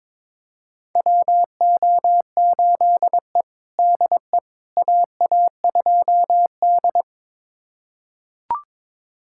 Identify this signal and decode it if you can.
Morse code, heard as a keyed tone, about 22 wpm, 700 Hz. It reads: WO8E DE AA2D
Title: USA Callsign DE Callsign - 4 Characters - 22wpm